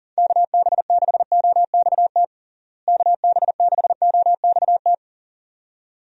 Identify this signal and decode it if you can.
Morse code, heard as a keyed tone, about 37 wpm, 700 Hz. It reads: KB6OXT KB6OXT
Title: USA Call Signs - 6 Characters - Repeated 2x - 40wpm